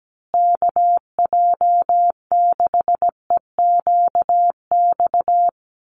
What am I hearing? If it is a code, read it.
Morse code, heard as a keyed tone, about 17 wpm, 700 Hz. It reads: KJ6EQX
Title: USA Call Signs Repeated 2x 17wpm